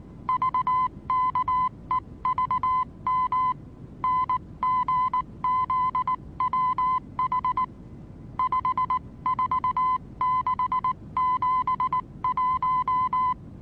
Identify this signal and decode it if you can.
Morse code, heard as a keyed tone, about 19 wpm, 1020 Hz. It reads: VKEVM NGZWH 54671
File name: repos / Morsekurs / DJ4UF Morsen26